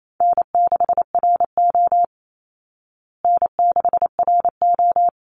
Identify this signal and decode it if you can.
Morse code, heard as a keyed tone, about 28 wpm, 700 Hz. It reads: N6RO N6RO
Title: Easy Call Signs Repeated 2x - 28wpm